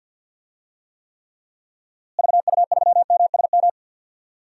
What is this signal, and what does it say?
Morse code, heard as a keyed tone, about 49 wpm, 700 Hz. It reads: VU3MSM
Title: DXCC-100 Call Signs - 6 Characters - Repeated 3x - 50wpm